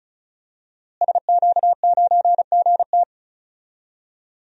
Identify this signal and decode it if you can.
Morse code, heard as a keyed tone, about 34 wpm, 700 Hz. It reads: SQ9GT
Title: DXCC-100 Call Signs - 5 Characters - Repeated 3x - 35wpm